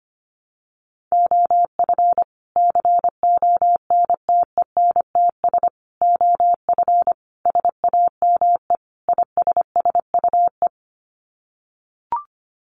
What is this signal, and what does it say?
Morse code, heard as a keyed tone, about 25 wpm, 700 Hz. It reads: OF CONTENTS OF SAME ISSUE
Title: Sets of 5 Words Encore2 25wpm